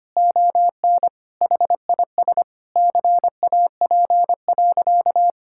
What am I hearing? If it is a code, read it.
Morse code, heard as a keyed tone, about 25 wpm, 700 Hz. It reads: ON HIS CAP.
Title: Sentences from Top 1600 Words 25wpm